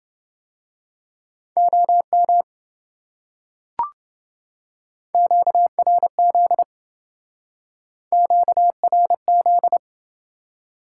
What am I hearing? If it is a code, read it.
Morse code, heard as a keyed tone, about 28 wpm, 700 Hz. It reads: OM QRZ QRZ